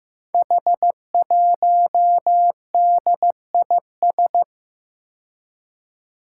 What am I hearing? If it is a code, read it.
Morse code, heard as a keyed tone, about 15 wpm, 700 Hz. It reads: H1DIS
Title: Five Challenging Letter-Number Combinations 15wpm